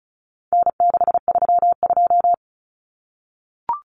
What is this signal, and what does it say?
Morse code, heard as a keyed tone, about 34 wpm, 700 Hz. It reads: N632
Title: Common Cut Numbers - 4-Digit 35wpm